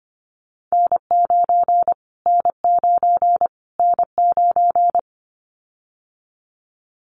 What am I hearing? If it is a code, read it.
Morse code, heard as a keyed tone, about 25 wpm, 700 Hz. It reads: N9 N9 N9
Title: Callsign Prefixes for Top 25 DXCC - Repeated 3x - 25wpm